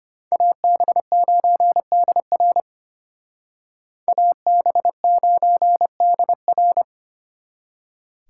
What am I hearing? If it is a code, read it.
Morse code, heard as a keyed tone, about 29 wpm, 700 Hz. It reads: AB9DR AB9DR
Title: Speed Racing - USA 2x2 Call Signs 25wpm